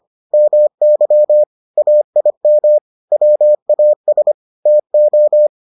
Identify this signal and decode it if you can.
Morse code, heard as a keyed tone, about 25 wpm, 600 Hz. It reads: MY AIM WAS TO